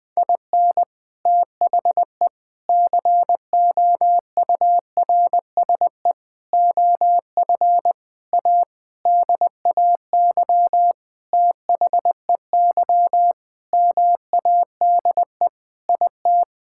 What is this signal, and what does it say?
Morse code, heard as a keyed tone, about 20 wpm, 700 Hz. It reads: IN THE COURSE OF A DAY THEY MADE IT
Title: Sentences with 4-Word Phrases - Spoken Beginning - 20wpm